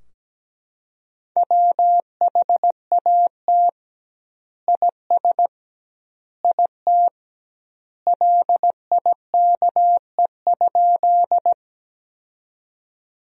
Morse code, heard as a keyed tone, about 17 wpm, 700 Hz. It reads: WHAT IS IT LIKE?
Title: 2x Word-Spacing - Sentences from Top 100 Words - 17wpm